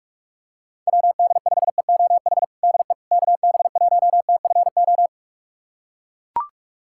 Morse code, heard as a keyed tone, about 42 wpm, 700 Hz. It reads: WD5EOH DE KB1TFO
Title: USA Callsign DE Callsign - 6 Characters - 45wpm